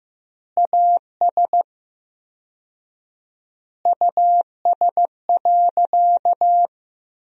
Morse code, heard as a keyed tone, about 15 wpm, 700 Hz. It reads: AS US.
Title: 4x Word-Spacing - Sentences from Top 500 Words - 15wpm